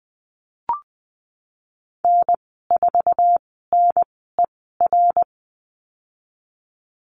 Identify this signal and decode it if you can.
Morse code, heard as a keyed tone, about 20 wpm, spaced as for 15 wpm, 700 Hz. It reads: N4NER
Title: 088 Lesson 15g - Number 5 - Call Signs 20wpm